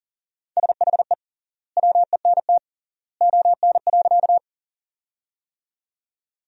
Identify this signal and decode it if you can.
Morse code, heard as a keyed tone, about 40 wpm, 700 Hz. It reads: SHE WENT ON.
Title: 3x Word-Spacing - Sentences from Top 200 Words - 40wpm